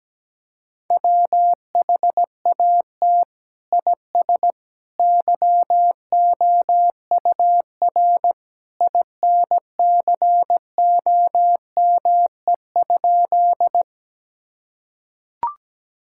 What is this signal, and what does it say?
Morse code, heard as a keyed tone, about 17 wpm, 700 Hz. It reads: WHAT IS YOUR INCOME?
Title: Sentences from Top 1000 Words - Review Part 5 - 17wpm